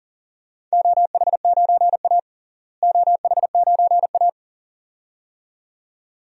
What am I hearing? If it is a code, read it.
Morse code, heard as a keyed tone, about 40 wpm, 700 Hz. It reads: OH9A OH9A
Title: DXCC-100 Call Signs - 4 Characters - Repeated 2x - 40wpm